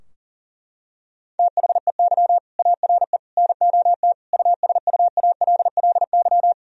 Morse code, heard as a keyed tone, about 40 wpm, 700 Hz. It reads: THEY ARE NOT USUALLY
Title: Sentences from Top 1000 Words - Review Part 3 - 40wpm